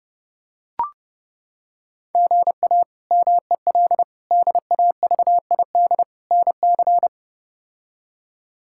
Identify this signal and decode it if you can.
Morse code, heard as a keyed tone, about 30 wpm, 700 Hz. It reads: GA MEL DAVID NC
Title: K1USN SST QSO Practice - 30wpm